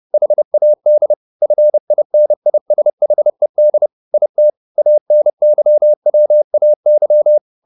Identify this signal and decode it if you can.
Morse code, heard as a keyed tone, about 30 wpm, 600 Hz. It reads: HAD FINISHED IT ANYWAY
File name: the old man and the sea 2 30wpm